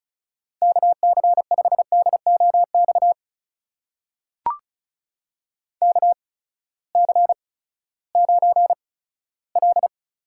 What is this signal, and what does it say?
Morse code, heard as a keyed tone, about 35 wpm, spaced as for 17 wpm, 700 Hz. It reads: KC5DOX KC9L
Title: USA Call Signs - 6 Characters - 8x Character Spacing - 35wpm